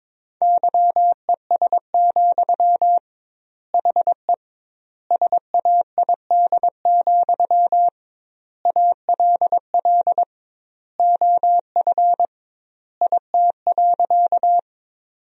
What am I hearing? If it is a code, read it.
Morse code, heard as a keyed tone, about 22 wpm, 700 Hz. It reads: YES, HE SAID, ALL OF IT.
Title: 2x Word-Spacing - Sentences from Top 500 Words - 22wpm